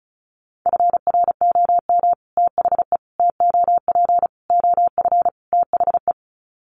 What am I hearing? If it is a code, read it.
Morse code, heard as a keyed tone, about 34 wpm, 700 Hz. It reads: FROM THE TOP OF THE
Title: Sets of 5 Words Encore 35wpm